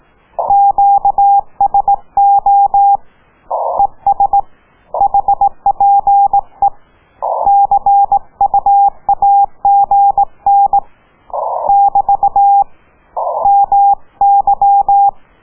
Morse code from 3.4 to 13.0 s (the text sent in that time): ES HPE CUAGN <BT>